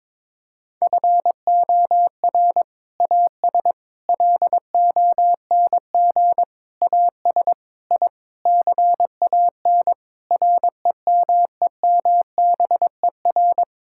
Morse code, heard as a keyed tone, about 22 wpm, 700 Hz. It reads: FOR AS LONG AS I CAN REMEMBER
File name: practice / 22wpm / Sentences with 8-Word Phrases - 22wpm